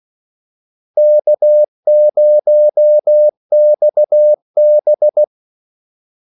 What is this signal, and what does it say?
Morse code, heard as a keyed tone, about 16 wpm, 600 Hz. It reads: K0XB
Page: Kaldesignaler 16 wpm